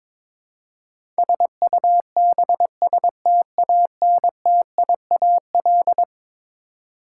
Morse code, heard as a keyed tone, about 22 wpm, 700 Hz. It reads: SUBSTANTIAL
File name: practice / 22wpm / Eleven Letter Words - 22wpm